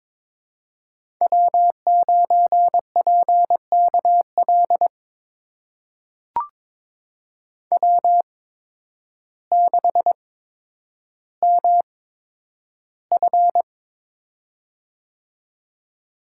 Morse code, heard as a keyed tone, about 22 wpm, spaced as for 10 wpm, 700 Hz. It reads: W9PKL W6MF
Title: USA Call Signs - 8x Character Spacing - 22wpm